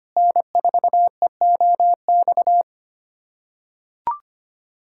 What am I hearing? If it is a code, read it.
Morse code, heard as a keyed tone, about 25 wpm, 700 Hz. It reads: N4EOX
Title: USA Callsign DE Callsign - 5 Characters - 25wpm